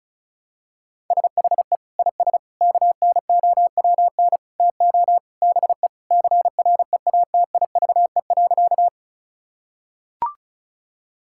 Morse code, heard as a keyed tone, about 34 wpm, 700 Hz. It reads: SHE IS KNOWN TO BE CREATIVE.
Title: Sentences from Top 1000 Words - Review Part 5 - 35wpm